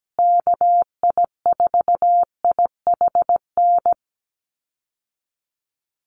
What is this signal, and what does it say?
Morse code, heard as a keyed tone, about 17 wpm, 700 Hz. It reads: KI4IHN
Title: USA Call Signs Repeated 3x 17wpm